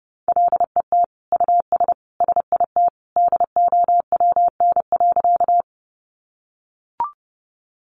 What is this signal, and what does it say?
Morse code, heard as a keyed tone, about 28 wpm, 700 Hz. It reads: LET US SIT DOWN.
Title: Sentences from Top 500 Words 30wpm